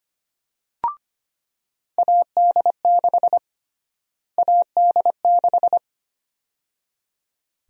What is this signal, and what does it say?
Morse code, heard as a keyed tone, about 25 wpm, 700 Hz. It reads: AD6 AD6
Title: USA Call Sign Prefixes - Repeated 2x - 25wpm